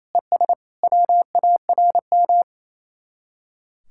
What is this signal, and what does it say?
Morse code, heard as a keyed tone, about 34 wpm, 700 Hz. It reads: ES WARM
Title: Speed-Racing Common QSO Phrases Encore - 28wpm